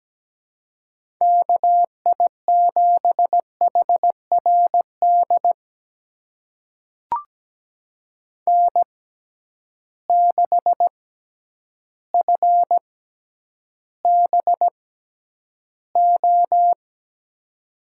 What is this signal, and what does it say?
Morse code, heard as a keyed tone, about 17 wpm, spaced as for 9 wpm, 700 Hz. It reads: KI7HRD N6FBO
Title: USA Call Signs - 6x Character Spacing - 17wpm